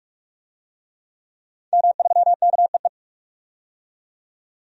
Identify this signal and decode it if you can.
Morse code, heard as a keyed tone, about 42 wpm, 700 Hz. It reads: M3KEE
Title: DXCC-100 Call Signs - 5 Characters - 45wpm